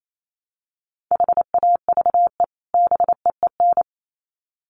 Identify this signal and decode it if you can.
Morse code, heard as a keyed tone, about 28 wpm, 700 Hz. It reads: HAVE BEEN
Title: Sets of 2 Words 28wpm